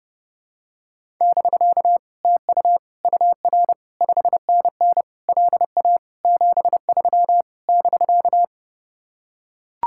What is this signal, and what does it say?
Morse code, heard as a keyed tone, about 32 wpm, 700 Hz. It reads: <BK> TU UR 5NN LA 73 <BK>